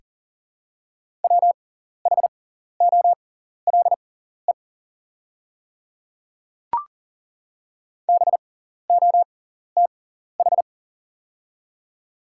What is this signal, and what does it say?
Morse code, heard as a keyed tone, about 40 wpm, spaced as for 14 wpm, 700 Hz. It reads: WHOLE BOTH
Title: 6x Character Spacing - Top 500 Words - 40wpm